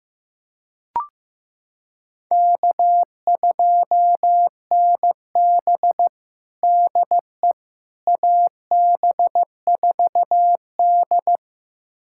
Morse code, heard as a keyed tone, about 15 wpm, 700 Hz. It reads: K2NB DE AB4D
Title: USA Callsign DE Callsign - 4 Characters - 15wpm